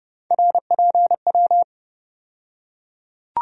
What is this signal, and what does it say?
Morse code, heard as a keyed tone, about 30 wpm, 700 Hz. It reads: RPW
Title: USA Call Sign Postfixes - 3 Letters 30wpm